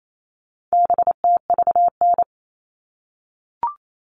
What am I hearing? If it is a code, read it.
Morse code, heard as a keyed tone, about 28 wpm, 700 Hz. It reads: BTVN
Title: All Cut Numbers - 4-Digit 28wpm